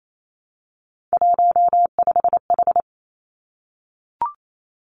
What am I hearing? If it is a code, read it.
Morse code, heard as a keyed tone, about 28 wpm, 700 Hz. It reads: 15H
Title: Three Challenging Letter-Number Combinations 28wpm